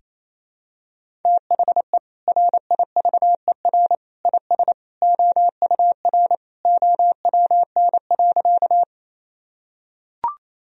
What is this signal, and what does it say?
Morse code, heard as a keyed tone, about 28 wpm, 700 Hz. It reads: THE RIVER IS OUR OWN.
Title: Sentences from Top 1000 Words - Review Part 1 - 28wpm